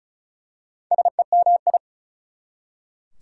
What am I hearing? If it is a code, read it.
Morse code, heard as a keyed tone, about 36 wpm, 690 Hz. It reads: SEMI
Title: Prefixes and Suffixes 35wpm